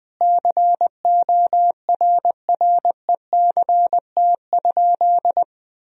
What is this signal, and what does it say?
Morse code, heard as a keyed tone, about 20 wpm, 700 Hz. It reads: CORRECT?
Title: Sentences from Top 1000 Words - Review Part 4 - 20wpm